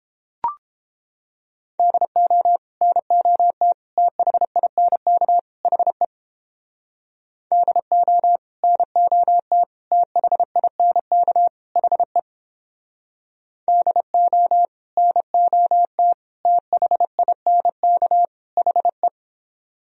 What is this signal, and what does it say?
Morse code, heard as a keyed tone, about 31 wpm, 700 Hz. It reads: DO NOT THINK HE DO NOT THINK HE DO NOT THINK HE